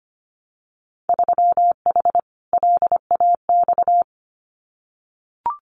Morse code, heard as a keyed tone, about 25 wpm, 700 Hz. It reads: 3H LAX
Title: Winter Field Day Exchanges 25wpm